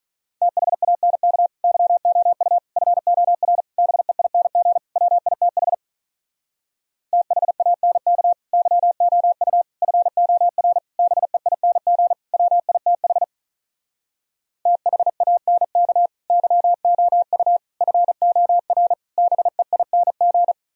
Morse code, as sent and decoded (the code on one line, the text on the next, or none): - .... .- -. -.- / -.-- --- ..- / ..-. --- .-. / -... .. -. --. / .-- .. - .... / - .... .- -. -.- / -.-- --- ..- / ..-. --- .-. / -... .. -. --. / .-- .. - .... / - .... .- -. -.- / -.-- --- ..- / ..-. --- .-. / -... .. -. --.
THANK YOU FOR BEING WITH THANK YOU FOR BEING WITH THANK YOU FOR BEING